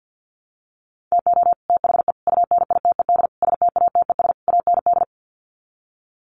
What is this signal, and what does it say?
Morse code, heard as a keyed tone, about 50 wpm, 700 Hz. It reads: TO THE UNITED STATES AND
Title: Sets of 5 Words Encore 50wpm